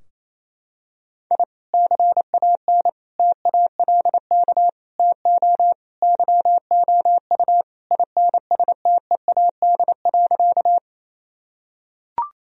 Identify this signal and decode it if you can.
Morse code, heard as a keyed tone, about 28 wpm, 700 Hz. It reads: I CAN TALK TO YOU INSTEAD.
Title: Sentences from Top 1000 Words - Review Part 4 - 28wpm